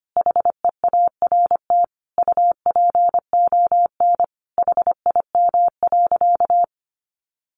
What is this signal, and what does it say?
Morse code, heard as a keyed tone, about 25 wpm, 700 Hz. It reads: HEART UPON HIM.
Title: Sentences from Top 1200 Words 25wpm